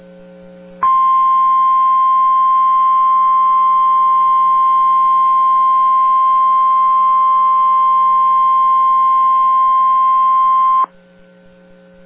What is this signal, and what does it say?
Emergency Alert System (SAME) tones (the 1050 Hz attention tone). It is heard from 0.8 to 10.9 s.